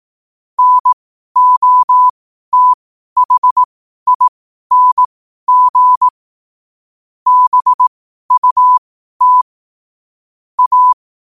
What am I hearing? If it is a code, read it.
Morse code, heard as a keyed tone, about 18 wpm, spaced as for 12 wpm, 1000 Hz. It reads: NOTHING BUT A
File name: qotdfiles / 12wpm